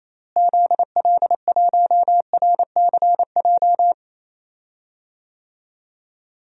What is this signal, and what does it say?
Morse code, heard as a keyed tone, about 27 wpm, 700 Hz. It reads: ZL1RCJ
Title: DXCC-100 Call Signs - 6 Characters - Repeated 2x - 28wpm